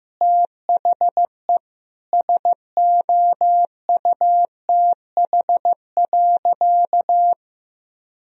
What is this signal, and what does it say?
Morse code, heard as a keyed tone, about 16 wpm, 700 Hz. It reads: THE SOUTH.
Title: Sentences from Top 1000 Words - Review Part 4 - 15wpm